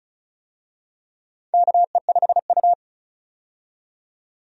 Morse code, heard as a keyed tone, about 34 wpm, 700 Hz. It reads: KE5U
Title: USA Call Signs - 4 Characters - 35wpm